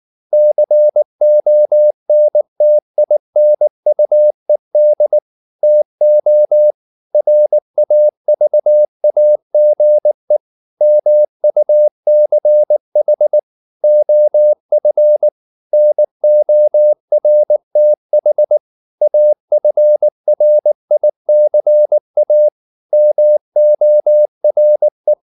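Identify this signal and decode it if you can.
Morse code, heard as a keyed tone, about 19 wpm, 600 Hz. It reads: CONTINUED TO RAVAGE MUCH OF NORTH AFRICA MORE
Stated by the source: Morse code at 19 wpm from CW med Gnister, reading Ken Follett's novel Never in English.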